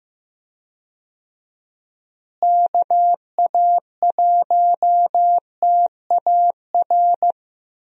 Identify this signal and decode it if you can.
Morse code, heard as a keyed tone, about 15 wpm, 700 Hz. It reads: KA1TAR
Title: USA Call Signs - 6 Characters - Repeated 2x - 15wpm